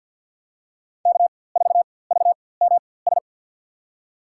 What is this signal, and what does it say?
Morse code, heard as a keyed tone, about 53 wpm, spaced as for 24 wpm, 700 Hz. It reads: K4VMS
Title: USA Call Signs - 5 Characters - 4x Character Spacing - 50wpm